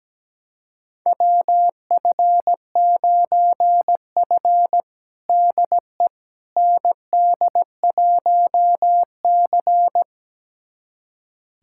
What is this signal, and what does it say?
Morse code, heard as a keyed tone, about 17 wpm, 700 Hz. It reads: WF9F DE ND1C
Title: USA Callsign DE Callsign - 4 Characters - 17wpm